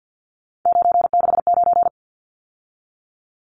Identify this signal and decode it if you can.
Morse code, heard as a keyed tone, about 52 wpm, 700 Hz. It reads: 969
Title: Numbers - 3-Digit 50wpm